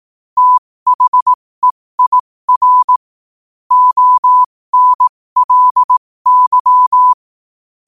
Morse code, heard as a keyed tone, about 18 wpm, spaced as for 15 wpm, 1000 Hz. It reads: THEIR ONLY
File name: qotdfiles / 15wpm